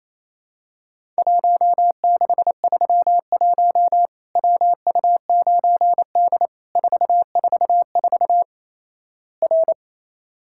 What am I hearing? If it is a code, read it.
Morse code, heard as a keyed tone, about 28 wpm, 700 Hz. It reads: 1631 WU9D 444 R
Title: Worked All Europe DX Contest - QTC Exchange - 28wpm